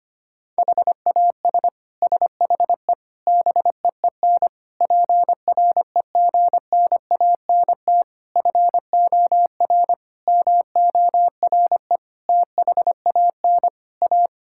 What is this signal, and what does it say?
Morse code, heard as a keyed tone, about 25 wpm, 700 Hz. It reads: HAS SHE BEEN PREGNANT FOR MORE THAN A
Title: Sentences from Top 1000 Words - Review Part 6 - 25wpm